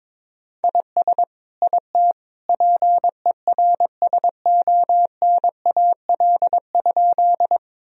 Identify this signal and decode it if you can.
Morse code, heard as a keyed tone, about 22 wpm, 700 Hz. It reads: IS IT PERSONAL?
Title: Sentences from Top 1000 Words - Review Part 4 - 22wpm